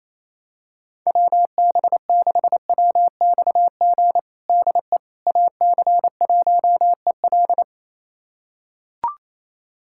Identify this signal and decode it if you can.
Morse code, heard as a keyed tone, about 28 wpm, 700 Hz. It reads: WB6WXG DE AC1EL
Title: USA Callsign DE Callsign 28wpm